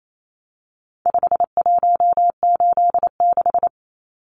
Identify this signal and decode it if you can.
Morse code, heard as a keyed tone, about 28 wpm, 700 Hz. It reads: 5186
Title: Numbers - 4-Digit 28wpm